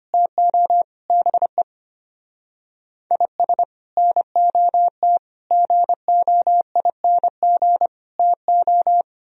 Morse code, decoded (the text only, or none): TO BE IS NOT GOING TO